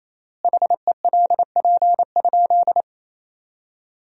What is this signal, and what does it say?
Morse code, heard as a keyed tone, about 28 wpm, 700 Hz. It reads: HELP?
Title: 4x Word-Spacing - Sentences from Top 200 Words - 28wpm